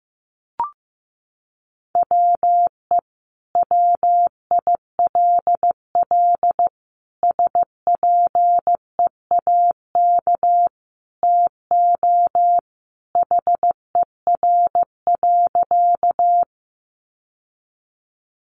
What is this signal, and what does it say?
Morse code, heard as a keyed tone, about 15 wpm, 700 Hz. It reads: WE WILL SPEAK TO HER.
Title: Sentences from Top 1100 Words 15wpm